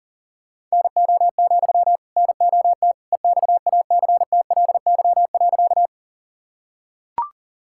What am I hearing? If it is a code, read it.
Morse code, heard as a keyed tone, about 40 wpm, 700 Hz. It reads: NO, NOT EXACTLY.
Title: Sentences from Top 1000 Words - Review Part 6 - 40wpm